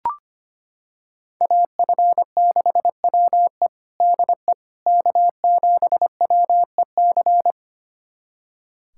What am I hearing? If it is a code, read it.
Morse code, heard as a keyed tone, about 25 wpm, 700 Hz. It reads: AF6WE DE K7WEC
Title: USA Callsign DE Callsign - 5 Characters - 25wpm